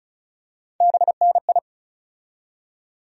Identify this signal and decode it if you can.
Morse code, heard as a keyed tone, about 36 wpm, 700 Hz. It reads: BNI
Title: USA Call Sign Postfixes - 3 Letters 35wpm